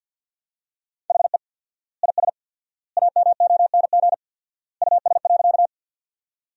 Morse code, heard as a keyed tone, about 47 wpm, 700 Hz. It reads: HE IS AMONG US.
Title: 4x Word-Spacing - Sentences from Top 500 Words - 50wpm